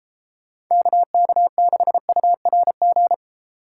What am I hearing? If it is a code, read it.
Morse code, heard as a keyed tone, about 34 wpm, 700 Hz. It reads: KK6URG